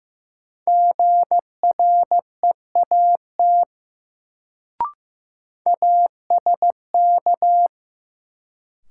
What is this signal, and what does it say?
Morse code, heard as a keyed tone, about 15 wpm, 700 Hz. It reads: GREAT ASK
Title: Sentences from Top 200 Words - New Word Drill - 15wpm